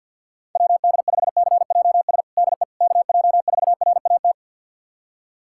Morse code, heard as a keyed tone, about 48 wpm, 700 Hz. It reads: WD5CJS DE KJ4RAT
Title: USA Callsign DE Callsign 50wpm